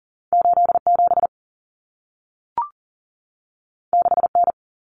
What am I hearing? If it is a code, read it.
Morse code, heard as a keyed tone, about 40 wpm, 700 Hz. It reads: 8 7 6 N
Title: Common Cut Numbers - 2-Digit 40wpm